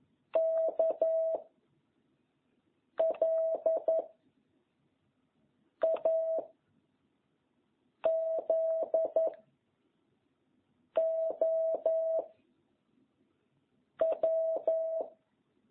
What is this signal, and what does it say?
Morse code, heard as a keyed tone, about 11 wpm, 660 Hz. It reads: K L A Z O W